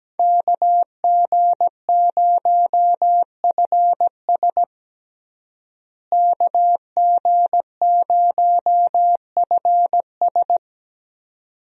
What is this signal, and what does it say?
Morse code, heard as a keyed tone, about 17 wpm, 700 Hz. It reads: KG0FS KG0FS
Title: USA Call Signs - 5 Characters - Repeated 2x - 17wpm